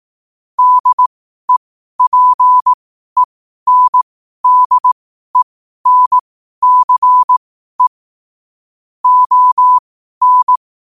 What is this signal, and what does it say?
Morse code, heard as a keyed tone, about 18 wpm, spaced as for 12 wpm, 1000 Hz. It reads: DEPENDENCE ON
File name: qotdfiles / 12wpm